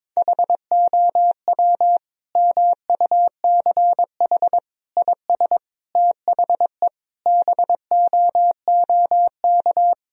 Morse code, heard as a keyed tone, about 22 wpm, 700 Hz. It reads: HOW MUCH IS THE BOOK
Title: Sentences from Top 1000 Words - Review Part 1 - 22wpm